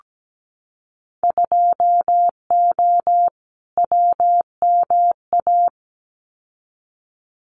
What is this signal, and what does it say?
Morse code, heard as a keyed tone, about 17 wpm, 700 Hz. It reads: 2O WMA
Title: Winter Field Day Exchanges Encore 17wpm